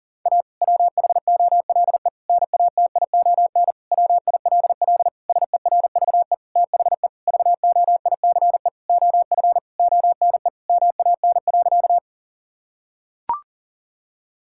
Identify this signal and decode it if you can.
Morse code, heard as a keyed tone, about 40 wpm, 700 Hz. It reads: A WHOLE NATION WILL SERVE THE VOICE OF ONE MAN.
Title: Sentences from Top 1000 Words - Review Part 6 - 40wpm